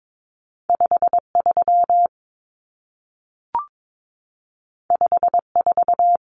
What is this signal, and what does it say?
Morse code, heard as a keyed tone, about 22 wpm, 700 Hz. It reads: 53 54
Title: Numbers - 2-Digit 22wpm